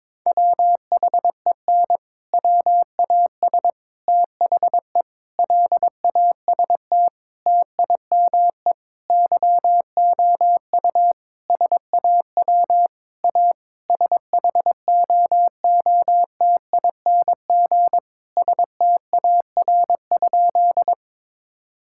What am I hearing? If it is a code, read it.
Morse code, heard as a keyed tone, about 22 wpm, 700 Hz. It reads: WHEN WAS THE LAST TIME YOU SAW A SHOOTING STAR?
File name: practice / 22wpm / Sentences with 8-Word Phrases - 22wpm